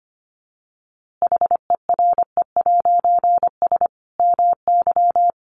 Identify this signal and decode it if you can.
Morse code, heard as a keyed tone, about 25 wpm, 700 Hz. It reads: HERE'S MY
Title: A Sound Of Thunder 25wpm